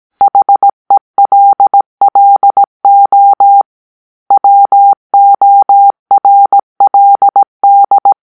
Morse code, heard as a keyed tone, about 18 wpm, 800 Hz. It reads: HELLO WORLD